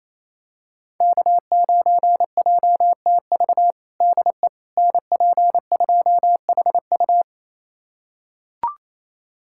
Morse code, heard as a keyed tone, about 28 wpm, 700 Hz. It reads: K9JTV DE NP2HU
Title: USA Callsign DE Callsign - 5 Characters - 28wpm